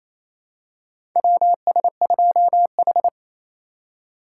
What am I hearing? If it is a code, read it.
Morse code, heard as a keyed tone, about 28 wpm, 700 Hz. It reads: WS2H
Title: USA Call Signs - 4 Characters - 4x Character Spacing - 28wpm